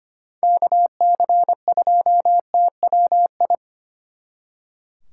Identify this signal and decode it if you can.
Morse code, heard as a keyed tone, about 31 wpm, 700 Hz. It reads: KC2TWI
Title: Speed Racing - USA Call Signs 25wpm